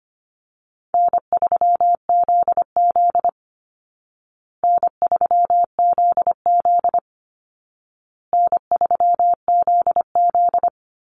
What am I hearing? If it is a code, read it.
Morse code, heard as a keyed tone, about 25 wpm, 700 Hz. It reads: N3ZZ N3ZZ N3ZZ
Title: Easy Call Signs Repeated 3x - 25wpm